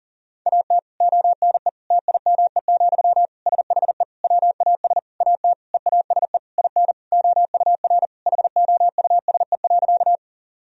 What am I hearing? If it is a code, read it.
Morse code, heard as a keyed tone, about 40 wpm, 700 Hz. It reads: AT ONE TIME, SHE WAS AT EASE IN OUR HOUSE.
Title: Sentences from Top 1000 Words - Review Part 1 - 40wpm